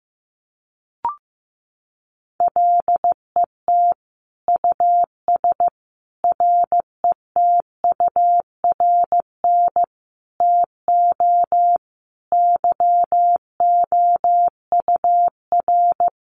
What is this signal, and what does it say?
Morse code, heard as a keyed tone, about 15 wpm, 700 Hz. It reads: LET US RETURN TO YOUR
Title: Sentences from Top 800 Words 15wpm